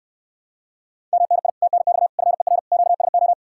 Morse code, heard as a keyed tone, about 66 wpm, 700 Hz. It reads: MNI TNX FER QSO